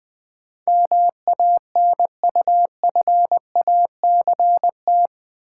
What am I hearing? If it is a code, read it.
Morse code, heard as a keyed tone, about 20 wpm, 700 Hz. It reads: MANUFACT
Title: Build A Word - 4 Letters At A Time - 11-Letter Words - 20wpm